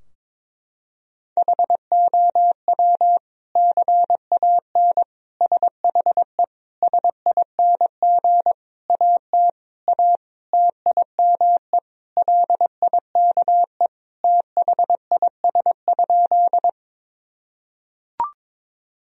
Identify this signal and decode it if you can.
Morse code, heard as a keyed tone, about 21 wpm, 700 Hz. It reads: HOW CAN SHE SING AT A TIME LIKE THIS?
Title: Sentences from Top 1000 Words - Review Part 2 - 22wpm